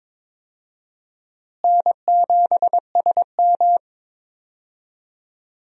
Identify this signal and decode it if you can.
Morse code, heard as a keyed tone, about 21 wpm, 700 Hz. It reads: N7SM
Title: USA Call Signs - 4 Characters - Repeated 2x - 22wpm